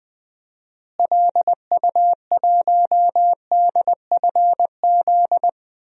Morse code, heard as a keyed tone, about 21 wpm, 700 Hz. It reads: LU1DFZ
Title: DXCC-100 Call Signs - 6 Characters - Repeated 2x - 20wpm